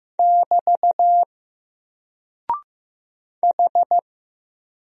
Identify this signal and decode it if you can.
Morse code, heard as a keyed tone, about 15 wpm, 700 Hz. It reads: <BT> H
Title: Single Character (Letter:Number:Punctuation:Prosign) Alt 15wpm